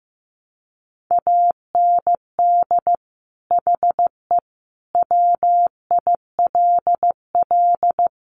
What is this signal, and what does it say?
Morse code, heard as a keyed tone, about 14 wpm, 700 Hz. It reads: AND HE WILL
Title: Sentences from Top 1400 Words 15wpm